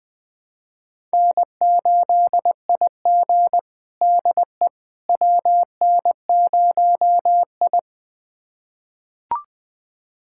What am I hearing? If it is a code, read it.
Morse code, heard as a keyed tone, about 20 wpm, 700 Hz. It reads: N8IG DE WN0I
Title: USA Callsign DE Callsign - 4 Characters - 20wpm